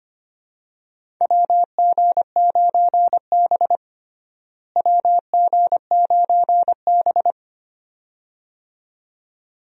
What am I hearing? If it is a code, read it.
Morse code, heard as a keyed tone, about 25 wpm, 700 Hz. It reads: WG9B WG9B
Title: USA Call Signs - 4 Characters - Repeated 2x - 25wpm